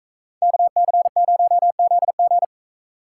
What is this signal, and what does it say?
Morse code, heard as a keyed tone, about 33 wpm, 700 Hz. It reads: KC0ZG
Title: Speed Racing - USA 2x2 Call Signs 28wpm